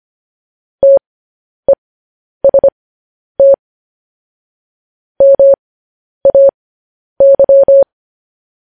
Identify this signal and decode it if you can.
Morse code, heard as a keyed tone, about 25 wpm, spaced as for 10 wpm, 550 Hz. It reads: TEST MAY
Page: news headlines in morse code at 10 wpm